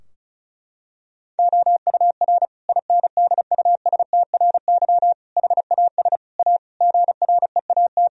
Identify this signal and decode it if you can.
Morse code, heard as a keyed tone, about 36 wpm, 700 Hz. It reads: OUR INDUSTRY HAS A GREAT
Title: Sentences from Top 1000 Words - Review Part 4 - 35wpm